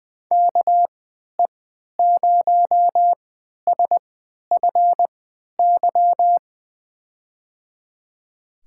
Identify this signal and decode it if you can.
Morse code, heard as a keyed tone, about 20 wpm, spaced as for 11 wpm, 700 Hz. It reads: KE0SFY
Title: USA Call Signs - 6 Characters - 3x Character Spacing - 20wpm